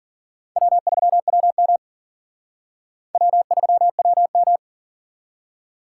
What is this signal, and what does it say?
Morse code, heard as a keyed tone, about 50 wpm, 700 Hz. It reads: W3WM W3WM
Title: Speed Racing - USA 1x2 Call Signs 40wpm